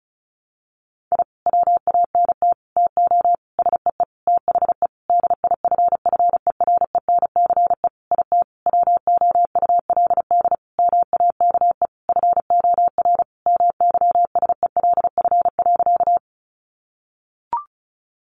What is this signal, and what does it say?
Morse code, heard as a keyed tone, about 34 wpm, 700 Hz. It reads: I WANT TO SEE THE DIFFERENCE IT WOULD MAKE FOR MYSELF.
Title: Sentences from Top 1300 Words 35wpm